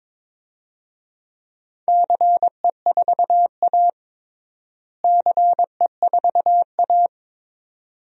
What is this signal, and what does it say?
Morse code, heard as a keyed tone, about 22 wpm, 700 Hz. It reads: CE4A CE4A
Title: DXCC-100 Call Signs - 4 Characters - Repeated 2x - 22wpm